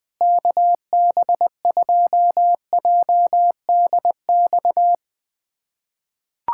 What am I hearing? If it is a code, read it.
Morse code, heard as a keyed tone, about 19 wpm, 700 Hz. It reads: KB2JDX
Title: USA Callsign DE Callsign 20wpm